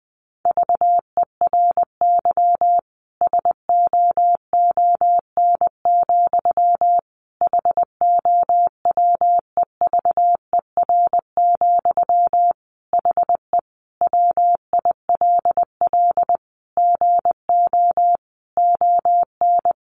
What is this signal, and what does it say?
Morse code, heard as a keyed tone, about 20 wpm, 700 Hz. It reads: VERY SOON, HOWEVER, HE WILL GO ON
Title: Sentences from Top 1400 Words 20wpm